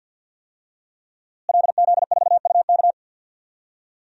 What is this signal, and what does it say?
Morse code, heard as a keyed tone, about 52 wpm, 700 Hz. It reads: LZ4UK
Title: DXCC-100 Call Signs - 5 Characters - Repeated 3x - 50wpm